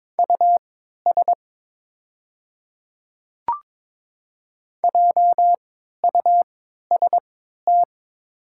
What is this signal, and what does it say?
Morse code, heard as a keyed tone, about 22 wpm, spaced as for 13 wpm, 690 Hz. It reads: US JUST
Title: 3x Character Spacing - Top 200 Words - 22wpm